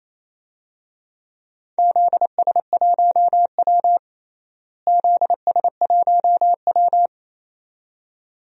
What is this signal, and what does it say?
Morse code, heard as a keyed tone, about 28 wpm, 700 Hz. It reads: ZS1W ZS1W
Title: DXCC-100 Call Signs - 4 Characters - Repeated 2x - 28wpm